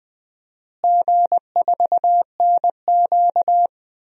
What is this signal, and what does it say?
Morse code, heard as a keyed tone, about 20 wpm, 700 Hz. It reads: G4NQ
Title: DXCC-100 Call Signs - 4 Characters - Repeated 2x - 20wpm